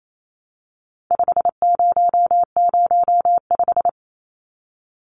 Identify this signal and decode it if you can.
Morse code, heard as a keyed tone, about 28 wpm, 700 Hz. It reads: 5005
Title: Numbers - 4-Digit 28wpm